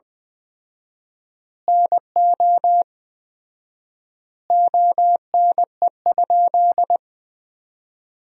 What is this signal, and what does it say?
Morse code, heard as a keyed tone, about 20 wpm, 700 Hz. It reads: NO ONE?
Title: 4x Word-Spacing - Sentences from Top 300 Words - 20wpm